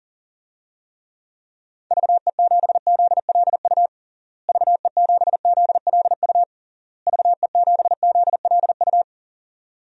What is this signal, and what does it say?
Morse code, heard as a keyed tone, about 40 wpm, 700 Hz. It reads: VE7ZLU VE7ZLU VE7ZLU
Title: DXCC-100 Call Signs - 6 Characters - Repeated 3x - 40wpm